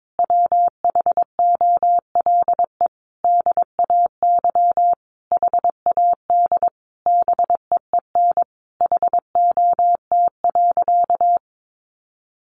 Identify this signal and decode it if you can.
Morse code, heard as a keyed tone, about 22 wpm, 700 Hz. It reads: WHOLE DAY HAD BEEN HOT.
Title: Sentences from Top 500 Words 22wpm